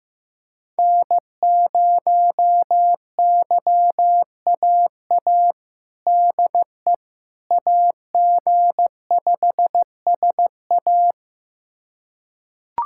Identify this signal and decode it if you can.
Morse code, heard as a keyed tone, about 16 wpm, 700 Hz. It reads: N0YAA DE AG5SA
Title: USA Callsign DE Callsign 15wpm